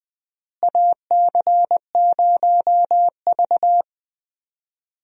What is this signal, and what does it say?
Morse code, heard as a keyed tone, about 21 wpm, 700 Hz. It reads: AC0V
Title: USA Call Signs - 4 Characters - Repeated 3x - 20wpm